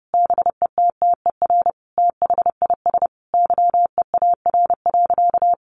Morse code, heard as a keyed tone, about 30 wpm, 700 Hz. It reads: BETTER THIS YEAR.
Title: Sentences from Top 1500 Words 30wpm